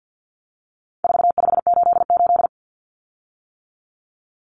Morse code, heard as a keyed tone, about 50 wpm, 700 Hz. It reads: 4588
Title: Common Cut Numbers - 4-Digit 50wpm